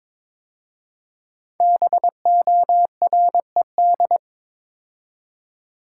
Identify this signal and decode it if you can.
Morse code, heard as a keyed tone, about 22 wpm, 690 Hz. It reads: BORED
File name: 5 Letter Words - First 3 Letters Given - 22wpm